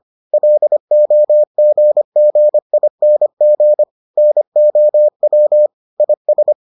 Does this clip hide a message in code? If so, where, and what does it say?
Yes: Morse code from 0.3 to 6.5 s, heard as a keyed tone, about 25 wpm, 600 Hz. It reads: LOGGING NOW IS